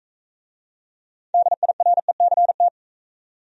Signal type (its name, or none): Morse code